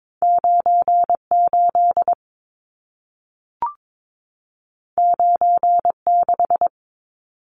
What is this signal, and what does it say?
Morse code, heard as a keyed tone, about 22 wpm, 700 Hz. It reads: 98 96
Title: Numbers - 2-Digit 22wpm